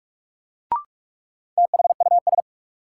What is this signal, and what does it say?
Morse code, heard as a keyed tone, about 46 wpm, 700 Hz. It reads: THUS
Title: Sentences from Top 800 Words - New Word Drill - 45wpm